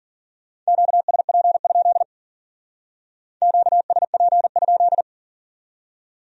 Morse code, heard as a keyed tone, about 51 wpm, 700 Hz. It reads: QSP? QSP?